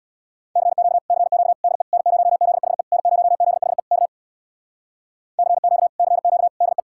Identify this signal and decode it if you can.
Morse code, heard as a keyed tone, about 67 wpm, 700 Hz. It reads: CQ CQ DE N0ZHE N0ZHE K CQ CQ DE